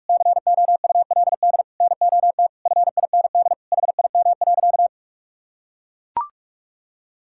Morse code, heard as a keyed tone, about 42 wpm, 700 Hz. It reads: COULD NOT FIND HIM.
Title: Sentences from Top 1000 Words - Review Part 5 - 45wpm